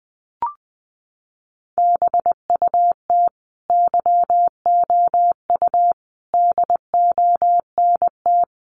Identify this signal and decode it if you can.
Morse code, heard as a keyed tone, about 20 wpm, 700 Hz. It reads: BUT YOU DONT
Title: Sentences from Top 600 Words 20wpm